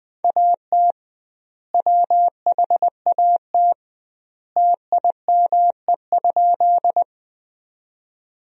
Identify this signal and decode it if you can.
Morse code, heard as a keyed tone, about 21 wpm, 700 Hz. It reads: AT WHAT TIME?
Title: 2x Word-Spacing - Sentences from Top 100 Words - 20wpm